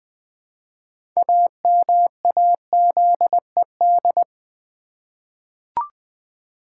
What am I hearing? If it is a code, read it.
Morse code, heard as a keyed tone, about 20 wpm, 700 Hz. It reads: AMAZED
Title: 6 Letter Words - First 3 Letters Given - 20wpm